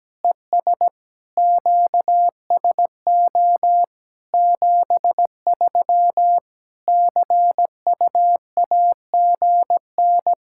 Morse code, heard as a keyed tone, about 17 wpm, 700 Hz. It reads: ES QSO 73 CUAGN
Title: K1USN SST QSO Practice 17wpm